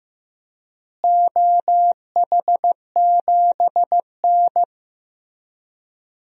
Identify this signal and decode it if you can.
Morse code, heard as a keyed tone, about 15 wpm, 700 Hz. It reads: OH7N
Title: DXCC-100 Call Signs - 4 Characters - Repeated 3x - 15wpm